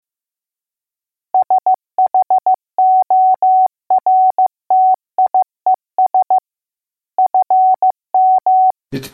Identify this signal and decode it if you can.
Morse code, heard as a keyed tone, about 15 wpm, 750 Hz. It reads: SHORTIES FM